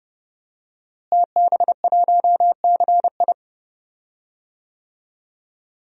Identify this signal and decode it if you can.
Morse code, heard as a keyed tone, about 30 wpm, 700 Hz. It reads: TB1CI
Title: DXCC-100 Call Signs - 5 Characters - Repeated 3x - 30wpm